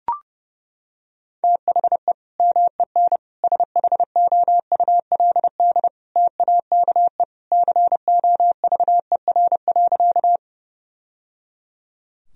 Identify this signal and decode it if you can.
Morse code, heard as a keyed tone, about 30 wpm, 700 Hz. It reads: THE MEN SHOULD TAKE COVER.
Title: Sentences from Top 1000 Words - Review Part 1 - 30wpm